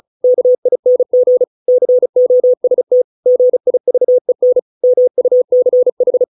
Morse code, heard as a keyed tone, about 34 wpm, 500 Hz. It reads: KING COST GIVEN MUCH